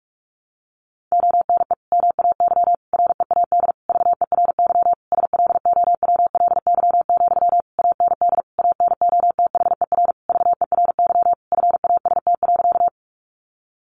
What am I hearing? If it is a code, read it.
Morse code, heard as a keyed tone, about 44 wpm, 700 Hz. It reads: ONE MAY READ VERY SLOWLY, AND ANOTHER VERY FAST.
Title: Sentences from Top 1100 Words 45wpm